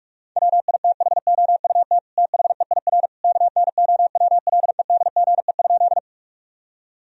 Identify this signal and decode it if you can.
Morse code, heard as a keyed tone, about 46 wpm, 700 Hz. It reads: WITHOUT THEIR KNOWLEDGE?
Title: Sentences from Top 1000 Words - Review Part 3 - 45wpm